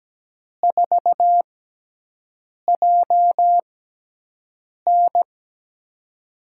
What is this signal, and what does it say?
Morse code, heard as a keyed tone, about 17 wpm, spaced as for 6 wpm, 700 Hz. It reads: 4JN
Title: USA Call Signs - 5 Characters - 6x Character Spacing - 17wpm